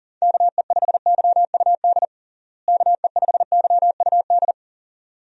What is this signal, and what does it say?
Morse code, heard as a keyed tone, about 40 wpm, 700 Hz. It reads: KE5YUD KE5YUD
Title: USA Call Signs - 6 Characters - Repeated 2x - 40wpm